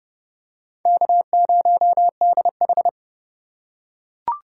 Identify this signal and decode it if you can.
Morse code, heard as a keyed tone, about 32 wpm, 700 Hz. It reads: K0DH
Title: USA Call Signs - 4 Characters - 4x Character Spacing - 30wpm